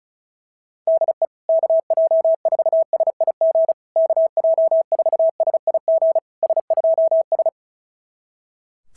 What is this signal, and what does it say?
Morse code, heard as a keyed tone, about 36 wpm, 650 Hz. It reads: DE KJ4SIG KJ4SIG S2S
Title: SOTA QSO Practice 35wpm